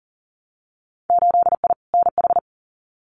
Morse code, heard as a keyed tone, about 40 wpm, 700 Hz. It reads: 8I NH
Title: Winter Field Day Exchanges Encore 40wpm